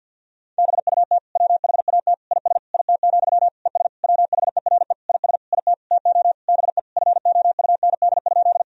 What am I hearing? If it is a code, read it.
Morse code, heard as a keyed tone, about 52 wpm, 700 Hz. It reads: BUT WHAT IS IT, ES WHERE IS IT TO BE FOUND?
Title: Sentences from Top 300 Words and Abbreviations 50wpm